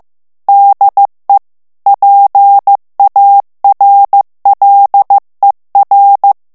Morse code, heard as a keyed tone, about 15 wpm, 800 Hz. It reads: DE PARLER
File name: FRANCAIS TEXTE CLAIR 2 VITESSE 14 MOTS MINUTES